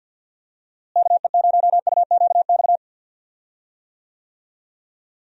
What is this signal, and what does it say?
Morse code, heard as a keyed tone, about 46 wpm, 700 Hz. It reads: KE0UQX
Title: USA Call Signs - 6 Characters - Repeated 3x - 50wpm